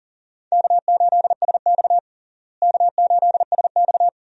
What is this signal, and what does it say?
Morse code, heard as a keyed tone, about 40 wpm, 700 Hz. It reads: K8SX K8SX
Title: USA Call Signs - 4 Characters - Repeated 3x - 40wpm